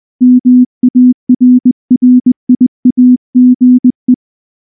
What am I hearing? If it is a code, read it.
Morse code, heard as a keyed tone, about 19 wpm, 250 Hz. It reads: MARRIAGE
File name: morse / morse-training-20WPM-5